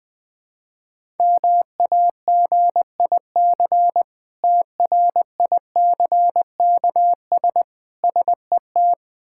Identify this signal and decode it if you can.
Morse code, heard as a keyed tone, about 21 wpm, 700 Hz. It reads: MAGIC TRICKS SET
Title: Difficult Words with Hints - 20wpm